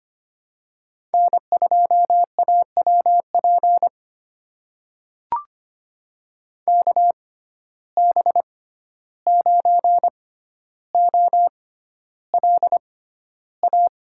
Morse code, heard as a keyed tone, about 25 wpm, spaced as for 12 wpm, 700 Hz. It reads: N2AWP KB9OLA
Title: USA Call Signs - 6x Character Spacing - 25wpm